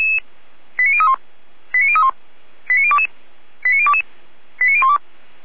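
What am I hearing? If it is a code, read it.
Five-tone selective call (ZVEI), heard as quick runs of five tones, tones of about 70 ms. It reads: 89031 89031 89022 89022 89012
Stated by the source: ZVEI five-tone call received via SDR.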